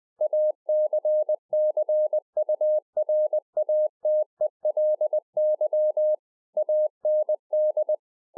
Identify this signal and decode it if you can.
Morse code, heard as a keyed tone, about 20 wpm, 610 Hz. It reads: ACCURATELY AND